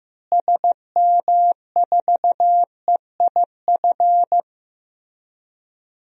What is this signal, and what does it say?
Morse code, heard as a keyed tone, about 15 wpm, 700 Hz. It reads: SM4EIF
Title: DXCC-100 Call Signs - 6 Characters - Repeated 3x - 15wpm